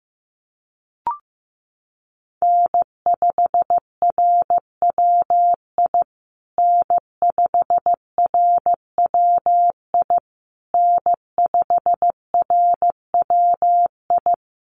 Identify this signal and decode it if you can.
Morse code, heard as a keyed tone, about 15 wpm, 700 Hz. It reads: N5RWI N5RWI N5RWI
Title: USA Call Signs Repeated 3x 15wpm